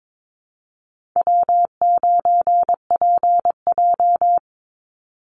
Morse code, heard as a keyed tone, about 22 wpm, 690 Hz. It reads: W9PJ
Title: Easy Call Signs Repeated 2x - 22wpm